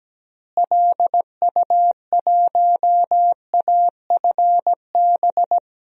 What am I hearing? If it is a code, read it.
Morse code, heard as a keyed tone, about 17 wpm, 700 Hz. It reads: LU1AFB
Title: DXCC-100 Call Signs - 6 Characters - Repeated 3x - 17wpm